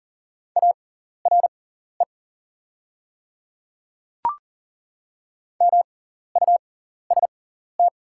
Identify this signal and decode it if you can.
Morse code, heard as a keyed tone, about 40 wpm, spaced as for 14 wpm, 690 Hz. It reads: ARE MUST
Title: 6x Character Spacing - Top 500 Words - 40wpm